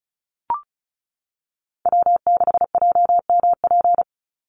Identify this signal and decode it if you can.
Morse code, heard as a keyed tone, about 34 wpm, 700 Hz. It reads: W6JMP
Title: Call Signs - 35wpm